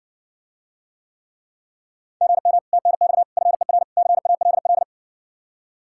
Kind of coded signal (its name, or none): Morse code